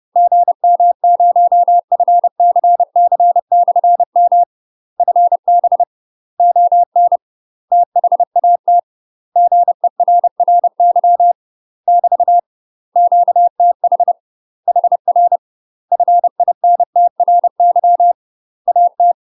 Morse code, heard as a keyed tone, about 30 wpm, 700 Hz. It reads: GM0FCC/M FB ON THAT GERRY <BT> QTH HR FINTRY AT
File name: UK MASTER QSOs TRAINING / 30WPM / UK MASTER QSOs 0051